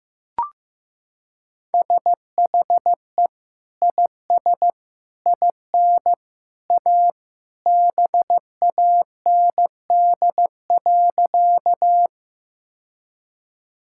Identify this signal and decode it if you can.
Morse code, heard as a keyed tone, about 15 wpm, 700 Hz. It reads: SHE IS IN A BAND.
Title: Sentences from Top 1600 Words 15wpm